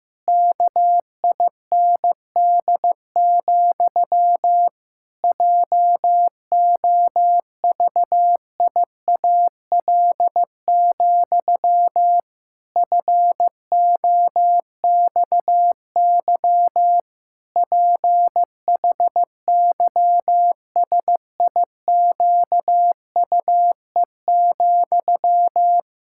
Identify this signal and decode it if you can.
Morse code, heard as a keyed tone, about 15 wpm, 700 Hz. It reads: KIND, JOVIAL, FOXY PHYSIQUE,
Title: Pangram Challenge v2 - 15wpm